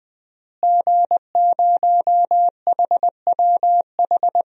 Morse code, heard as a keyed tone, about 20 wpm, 700 Hz. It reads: G0HWH
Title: DXCC-100 Call Signs - 5 Characters - Repeated 3x - 20wpm